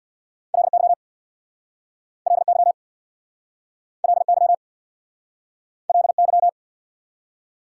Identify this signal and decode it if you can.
Morse code, heard as a keyed tone, about 69 wpm, 690 Hz. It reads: LY LY LY LY